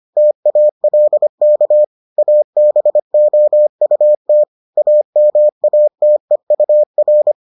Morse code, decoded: TALK ABOUT AMATEUR

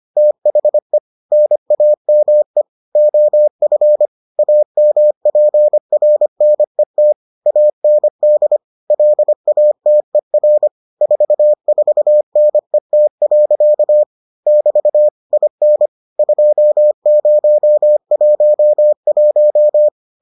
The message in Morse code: THE NAME OF AMPRNET AND LATER 44NET. <BT> IN 2011

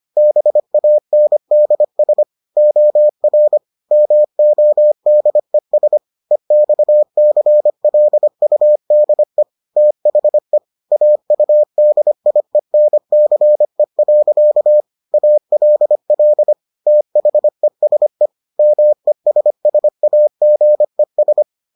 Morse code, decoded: BANDS OR MODES EXCLUDE THE AUDIENCE. ALL THESE MESSAGES

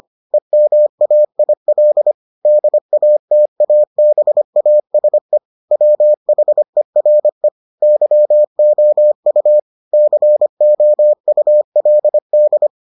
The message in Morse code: EMAIL DATABASE WHERE YOU COULD

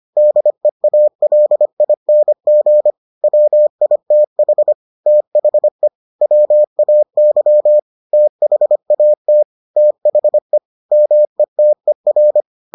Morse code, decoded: DEALING WITH THE WAY THAT THE METER

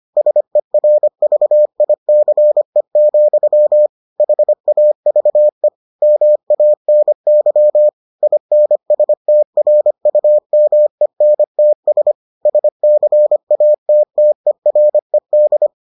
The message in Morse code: SERVICE, HAVE MANY INSTRUMENTS SCATTERED

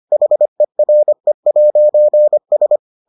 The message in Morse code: HERE'S

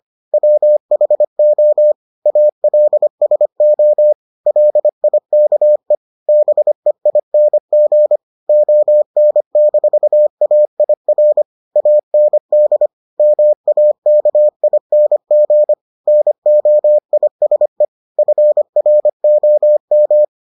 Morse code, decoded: WHO ALSO LIKE BEING ON-AIR AND MAKING NOISE FROM

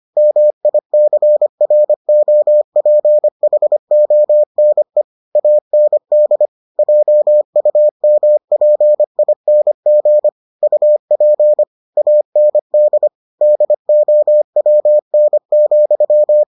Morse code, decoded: MICROPHONE AND JUMPING UP AND DOWN,

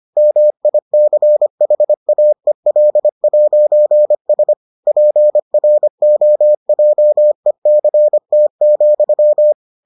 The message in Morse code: MICHAEL'S PROJECT,

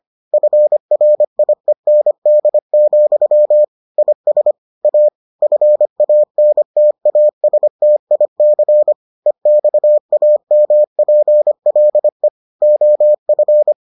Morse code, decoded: FRIEND, IS A FANTASTIC EXAMPLE OF

